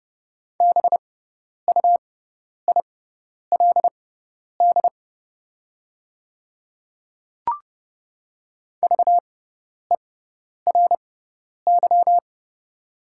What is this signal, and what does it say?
Morse code, heard as a keyed tone, about 30 wpm, spaced as for 10 wpm, 690 Hz. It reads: BUILD VERY
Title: 6x Character Spacing - Top 200 Words - 30wpm